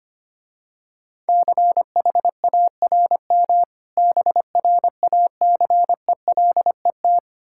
Morse code, heard as a keyed tone, about 26 wpm, 690 Hz. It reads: CHARM BRACELET
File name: practice / 25wpm / Difficult Words - Speed-Racing with Hints - 25wpm